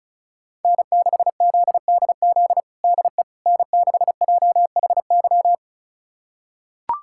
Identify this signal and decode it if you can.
Morse code, heard as a keyed tone, about 36 wpm, 700 Hz. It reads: N6ZDZ DE N6JHY
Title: USA Callsign DE Callsign - 5 Characters - 35wpm